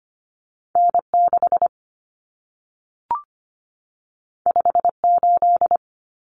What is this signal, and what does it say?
Morse code, heard as a keyed tone, about 25 wpm, 700 Hz. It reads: N6 58
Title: Common Cut Numbers - 2-Digit 25wpm